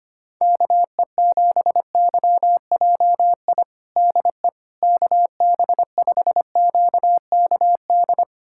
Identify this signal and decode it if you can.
Morse code, heard as a keyed tone, about 25 wpm, 700 Hz. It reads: KE7YJI DE KB5QKD
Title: USA Callsign DE Callsign - 6 Characters - 25wpm